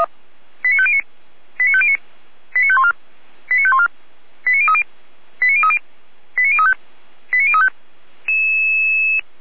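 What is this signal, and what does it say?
Five-tone selective call (ZVEI), heard as quick runs of five tones, tones of about 70 ms. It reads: 89509 89509 89514 89514 89030 89030 89035 89035
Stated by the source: ZVEI five-tone selective call received by SDR, served by Feuerwehr Orlamünde.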